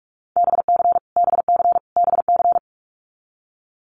Morse code, heard as a keyed tone, about 44 wpm, 700 Hz. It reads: BC BC BC
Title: Canadian Provinces and Territories - Repeated 3x - 45wpm